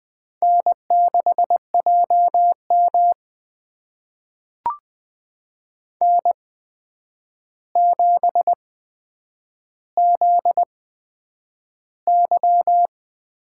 Morse code, heard as a keyed tone, about 20 wpm, spaced as for 9 wpm, 700 Hz. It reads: N6JM N7ZY
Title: USA Call Signs - 4 Characters - 8x Character Spacing - 20wpm